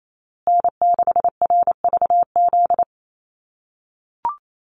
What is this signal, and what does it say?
Morse code, heard as a keyed tone, about 28 wpm, 700 Hz. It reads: N6RVZ
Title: USA Call Signs Repeated 3x 28wpm